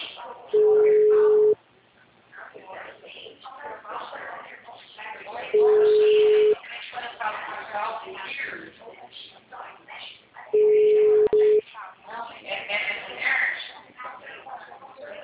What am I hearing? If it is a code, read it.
Ringback tone (Europe, 425 Hz), with speech.
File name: out- -2001- - - .33058